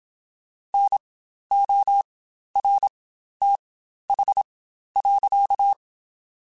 Morse code, heard as a keyed tone, about 27 wpm, spaced as for 12 wpm, 800 Hz. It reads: NORTH.
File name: Spaced Characters / Spaced Characters-mp3s / Weather4 24 sp @25